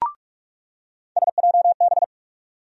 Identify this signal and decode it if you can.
Morse code, heard as a keyed tone, about 46 wpm, 700 Hz. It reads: SJB
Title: USA Call Sign Postfixes - 3 Letters 45wpm